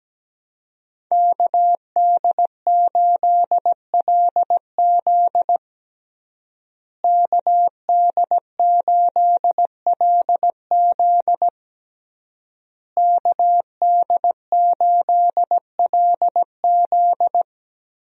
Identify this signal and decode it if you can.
Morse code, heard as a keyed tone, about 17 wpm, 700 Hz. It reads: KD8LZ KD8LZ KD8LZ
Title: USA Call Signs - 5 Characters - Repeated 3x - 17wpm